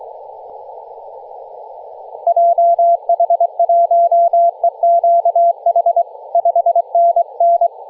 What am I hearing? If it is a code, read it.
Morse code, heard as a keyed tone, about 22 wpm, 680 Hz. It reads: JH1EQH 5NN
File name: posts / 2016-nov-13-0348z-t30jy-28-034mhz-cw